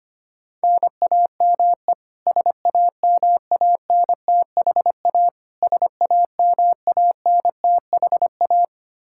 Morse code, heard as a keyed tone, about 25 wpm, 700 Hz. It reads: NAME SAMANTHA SAMANTHA
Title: Practice QSO - Exchange 1 - Beginning - 25wpm